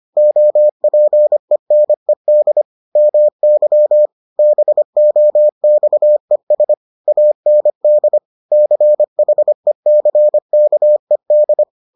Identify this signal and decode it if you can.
Morse code, heard as a keyed tone, about 25 wpm, 600 Hz. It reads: OPENED MY BOXES AND CHECKED